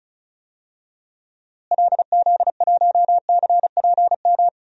Morse code, heard as a keyed tone, about 34 wpm, 700 Hz. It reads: LZ1CPM
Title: DXCC-100 Call Signs - 6 Characters - Repeated 3x - 35wpm